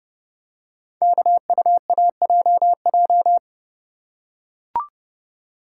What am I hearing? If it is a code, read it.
Morse code, heard as a keyed tone, about 30 wpm, 690 Hz. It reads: KUAJJ
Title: Code Groups - 5 Letters - 30wpm